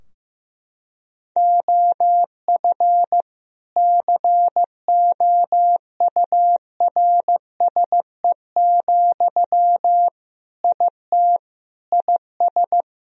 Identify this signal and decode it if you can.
Morse code, heard as a keyed tone, about 15 wpm, 700 Hz. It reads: OF COURSE, IT IS
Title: Sentences from Top 1000 Words - Review Part 2 - 15wpm